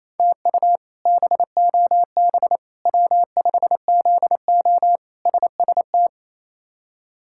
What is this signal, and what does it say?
Morse code, heard as a keyed tone, about 28 wpm, 700 Hz. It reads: TU BOB W5ZO SST
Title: K1USN SST QSO Practice - 28wpm